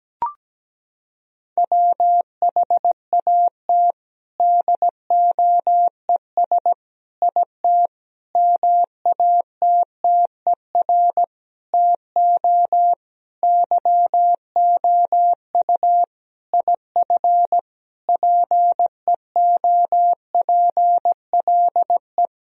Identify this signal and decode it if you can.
Morse code, heard as a keyed tone, about 17 wpm, 700 Hz. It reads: WHAT DOES IT MATTER TO YOU IF PEOPLE
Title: Sentences from Top 1000 Words - Review Part 3 - 17wpm